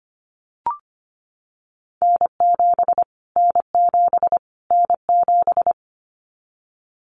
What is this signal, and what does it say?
Morse code, heard as a keyed tone, about 25 wpm, 700 Hz. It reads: N7 N7 N7
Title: Callsign Prefixes for Top 25 DXCC - Repeated 3x - 25wpm